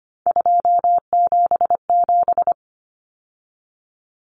Morse code, heard as a keyed tone, about 25 wpm, 700 Hz. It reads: 277
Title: Common Cut Numbers - 3-Digit 25wpm